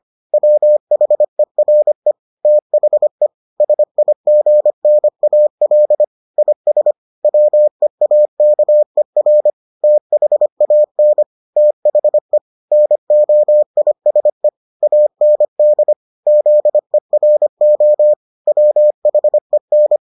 Morse code, heard as a keyed tone, about 25 wpm, 600 Hz. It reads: WHERE THE SIGNAL IS WEAKER THAN THE NOISE AND ZERO WHEN